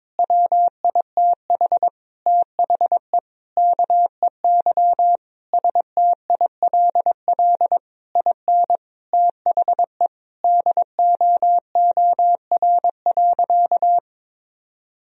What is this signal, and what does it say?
Morse code, heard as a keyed tone, about 22 wpm, 700 Hz. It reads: WITH THE KEY STILL IN THE DOOR.
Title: Sentences from Top 1000 Words - Review Part 4 - 22wpm